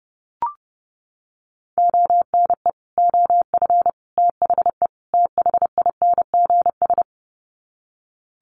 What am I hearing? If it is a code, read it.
Morse code, heard as a keyed tone, about 30 wpm, 700 Hz. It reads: ONE OF THE THINGS
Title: Sets of 4 Words 30wpm